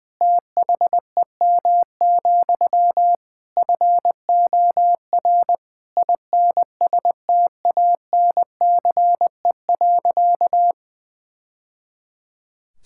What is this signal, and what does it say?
Morse code, heard as a keyed tone, about 21 wpm, 700 Hz. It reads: THEM, FOR INSTANCE.
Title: Sentences from Top 1000 Words - Review Part 5 - 20wpm